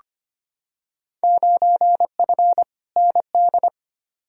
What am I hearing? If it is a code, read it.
Morse code, heard as a keyed tone, about 25 wpm, 700 Hz. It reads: 9F ND
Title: Field Day Exchanges Encore - 25wpm